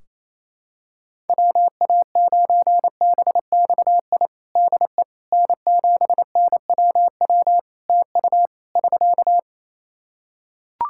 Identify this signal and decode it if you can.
Morse code, heard as a keyed tone, about 28 wpm, 700 Hz. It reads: WA9BXI DE N7NWW TU <SK>
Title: Practice QSO - Exchange 4 - Ending - 28wpm